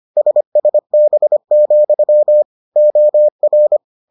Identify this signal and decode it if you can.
Morse code, heard as a keyed tone, about 25 wpm, 600 Hz. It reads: SSB, OR